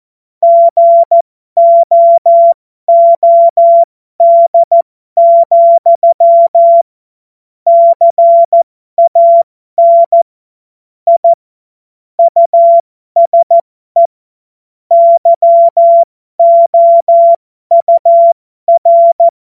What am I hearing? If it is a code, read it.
Morse code, heard as a keyed tone, about 14 wpm, spaced as for 12 wpm, 680 Hz. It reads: GOOD, CAN I USE YOUR
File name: Morsekurs / Witze